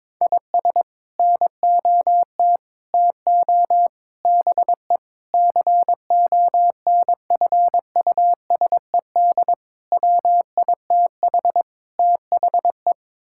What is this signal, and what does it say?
Morse code, heard as a keyed tone, about 22 wpm, 700 Hz. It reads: IS NOT TO BE CONFUSED WITH THE
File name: practice / 22wpm / Sets of 7 Words - 22wpm